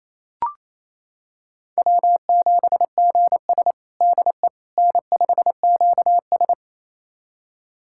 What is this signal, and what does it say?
Morse code, heard as a keyed tone, about 27 wpm, 690 Hz. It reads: W7GS DE N5QS
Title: USA Callsign DE Callsign - 4 Characters - 28wpm